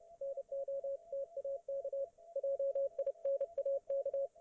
Morse code, heard as a keyed tone, about 30 wpm, 550 Hz, with radio noise. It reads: NOTAK JINAK